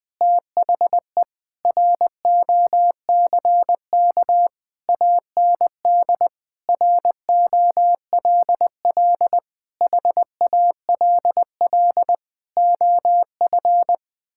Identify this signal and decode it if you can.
Morse code, heard as a keyed tone, about 20 wpm, 700 Hz. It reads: THE ROCK AND ROLL HALL OF
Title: Sentences with 6-Word Phrases - 20wpm